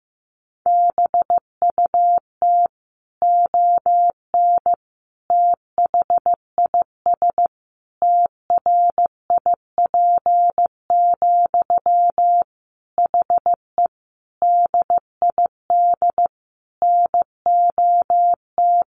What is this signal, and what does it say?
Morse code, heard as a keyed tone, about 15 wpm, 700 Hz. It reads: BUT ON THIS TRIP, HE DID NOT
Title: Sentences from Top 1400 Words 15wpm